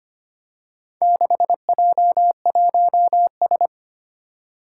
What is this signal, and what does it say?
Morse code, heard as a keyed tone, about 25 wpm, 700 Hz. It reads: 6J1S
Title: Four Challenging Letter-Number Combinations 25wpm